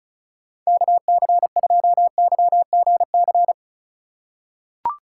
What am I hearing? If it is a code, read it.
Morse code, heard as a keyed tone, about 34 wpm, 700 Hz. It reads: KC2YGC
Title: USA Call Signs - 6 Characters - 4x Character Spacing - 35wpm